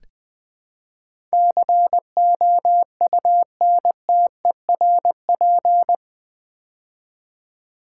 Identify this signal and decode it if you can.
Morse code, heard as a keyed tone, about 20 wpm, 700 Hz. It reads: COUNTERP
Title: Build A Word - 4 Letters At A Time - 11-Letter Words - 20wpm